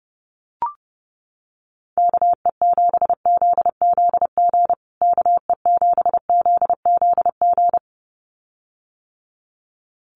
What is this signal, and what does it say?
Morse code, heard as a keyed tone, about 30 wpm, 700 Hz. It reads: KE7ZZG KE7ZZG
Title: USA Call Signs Repeated 2x 30wpm